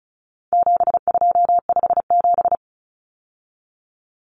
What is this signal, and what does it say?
Morse code, heard as a keyed tone, about 34 wpm, 700 Hz. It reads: 7257
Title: Numbers - 4-Digit 35wpm